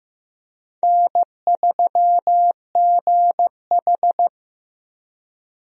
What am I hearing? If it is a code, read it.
Morse code, heard as a keyed tone, about 15 wpm, 700 Hz. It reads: N3GH
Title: USA Call Signs - 4 Characters - Repeated 3x - 15wpm